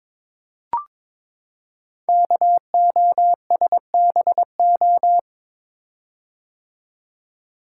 Morse code, heard as a keyed tone, about 23 wpm, 700 Hz. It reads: KOSBO
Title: Code Groups - 5 Letters - 22wpm